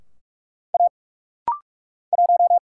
Morse code, heard as a keyed tone, about 46 wpm, 700 Hz. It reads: A 1
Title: Single Letter-Number Alt - Rapid-Fire 45wpm